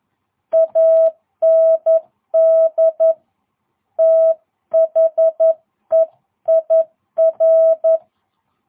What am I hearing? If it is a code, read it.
Morse code, heard as a keyed tone, about 11 wpm, 660 Hz. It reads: AND THEIR